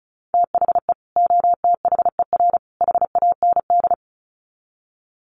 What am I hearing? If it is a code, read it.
Morse code, heard as a keyed tone, about 34 wpm, 700 Hz. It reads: THE OTHER HAND
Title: Sets of 3 Words 35wpm